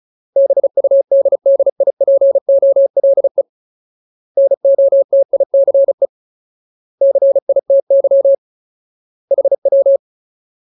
Morse code, heard as a keyed tone, about 34 wpm, 550 Hz. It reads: BUDDIPOLE NOTICE CITY HW
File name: Ultimate Word List / 35WPM Ultimate